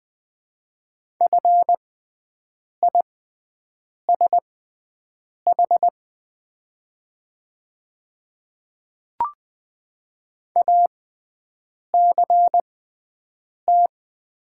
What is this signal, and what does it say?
Morse code, heard as a keyed tone, about 20 wpm, spaced as for 7 wpm, 700 Hz. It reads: FISH ACT
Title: 6x Character Spacing - Top 300 Words - 20wpm